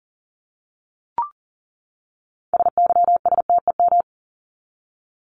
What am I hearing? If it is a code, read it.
Morse code, heard as a keyed tone, about 40 wpm, 700 Hz. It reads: SYSTEM
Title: ES:S Suffix - Top 100 Words with Base Words 40wpm